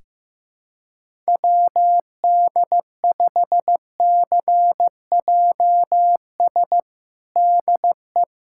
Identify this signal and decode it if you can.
Morse code, heard as a keyed tone, about 15 wpm, 700 Hz. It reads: WD5CJS DE
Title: USA Callsign DE Callsign 15wpm